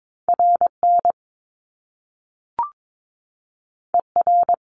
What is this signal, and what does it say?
Morse code, heard as a keyed tone, about 22 wpm, 700 Hz. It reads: RN ER
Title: Top Two Letter Combinations in Words 22wpm